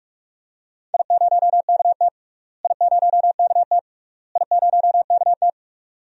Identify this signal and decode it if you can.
Morse code, heard as a keyed tone, about 43 wpm, 700 Hz. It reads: I0KT I0KT I0KT
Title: DXCC-100 Call Signs - 4 Characters - Repeated 3x - 45wpm